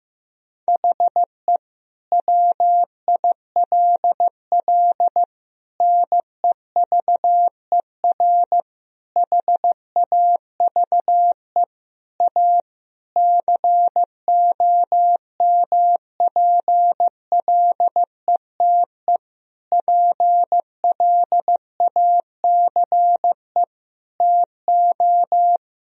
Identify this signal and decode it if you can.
Morse code, heard as a keyed tone, about 15 wpm, 700 Hz. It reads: HE WILL NEVER HAVE A COMPLETE PLACE TO
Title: Sentences from Top 1000 Words - Review Part 2 - 15wpm